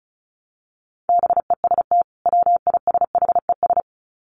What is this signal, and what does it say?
Morse code, heard as a keyed tone, about 34 wpm, 700 Hz. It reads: BEST WISHES
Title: Common QSO Phrases 35wpm